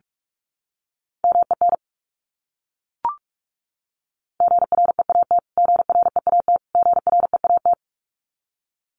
Morse code, heard as a keyed tone, about 44 wpm, 700 Hz. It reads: MEN GREAT GREAT GREAT
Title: Top 300 Words Repeated 45wpm